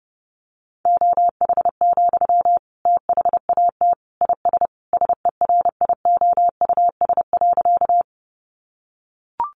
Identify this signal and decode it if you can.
Morse code, heard as a keyed tone, about 30 wpm, 700 Hz. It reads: OH, THAT IS SERIOUS.
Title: Sentences from Top 1100 Words 30wpm